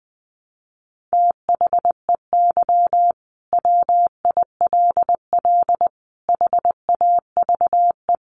Morse code, heard as a keyed tone, about 20 wpm, 700 Hz. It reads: THEY WILL HAVE
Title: Sentences from Top 800 Words 20wpm